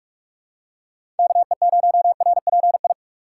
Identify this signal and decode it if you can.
Morse code, heard as a keyed tone, about 41 wpm, 700 Hz. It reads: KE0RPI
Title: USA Call Signs - 6 Characters - 3x Character Spacing - 45wpm